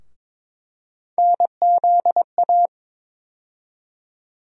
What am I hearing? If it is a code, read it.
Morse code, heard as a keyed tone, about 22 wpm, 700 Hz. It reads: NZA
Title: USA Call Sign Postfixes - 3 Letters 22wpm